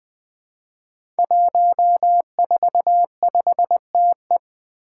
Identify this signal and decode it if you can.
Morse code, heard as a keyed tone, about 20 wpm, 700 Hz. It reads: 145TE
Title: Five Challenging Letter-Number Combinations 20wpm